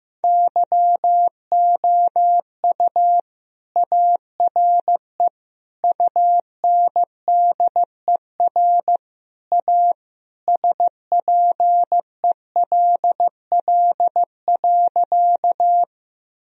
Morse code, heard as a keyed tone, about 15 wpm, 700 Hz. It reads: YOU ARE UNDER A SPELL.
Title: Sentences from Top 1000 Words - Review Part 1 - 15wpm